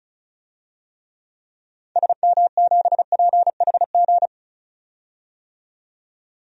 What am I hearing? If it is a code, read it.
Morse code, heard as a keyed tone, about 36 wpm, 700 Hz. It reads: SM7PHG
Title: DXCC-100 Call Signs - 6 Characters - 35wpm